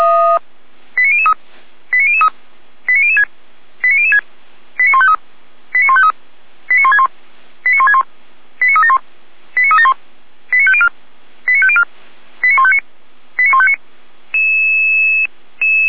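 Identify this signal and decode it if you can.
Five-tone selective call (ZVEI), heard as quick runs of five tones, tones of about 70 ms. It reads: 89003 89003 89006 89006 89153 89153 89161 89161 89371 89371 89504 89504 89159 89159
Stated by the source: ZVEI five-tone selective call received on SDR, server of feuerwehr-orlamuende (fire brigade Orlamünde).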